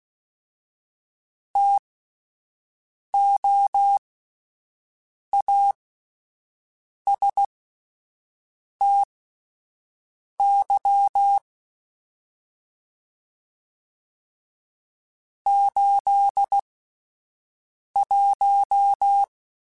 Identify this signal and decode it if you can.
Morse code, heard as a keyed tone, about 16 wpm, spaced as for 5 wpm, 800 Hz. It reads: TOASTY 81
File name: Spaced Characters / Spaced Characters-mp3s / Weather5 12 sp @10